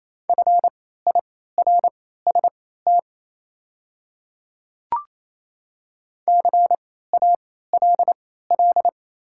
Morse code, heard as a keyed tone, about 27 wpm, spaced as for 16 wpm, 700 Hz. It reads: FIRST CALL
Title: 3x Character Spacing - Top 100 Words - 28wpm